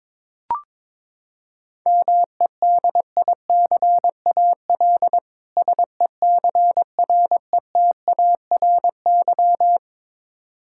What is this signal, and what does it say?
Morse code, heard as a keyed tone, about 22 wpm, 700 Hz. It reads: MEDICAL SECRETARY
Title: Practice QSO - Exchange 3 - More Info - 22wpm